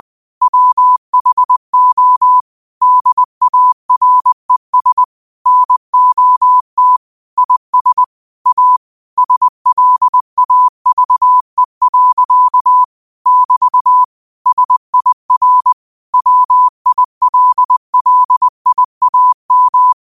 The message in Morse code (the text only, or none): WHO DARES NOT IS A SLAVE. <BT> SIR WILLIAM